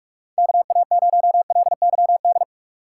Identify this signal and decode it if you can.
Morse code, heard as a keyed tone, about 41 wpm, 700 Hz. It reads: KA0LYD
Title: USA Call Signs - 6 Characters - 2x Character Spacing - 45wpm